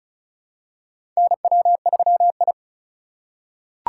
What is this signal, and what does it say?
Morse code, heard as a keyed tone, about 34 wpm, 700 Hz. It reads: NW3I
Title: USA Call Signs - 4 Characters - 3x Character Spacing - 35wpm